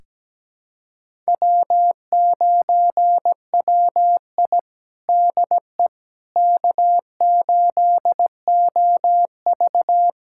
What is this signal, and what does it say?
Morse code, heard as a keyed tone, about 17 wpm, 700 Hz. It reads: W9WI DE K8OV
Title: USA Callsign DE Callsign - 4 Characters - 17wpm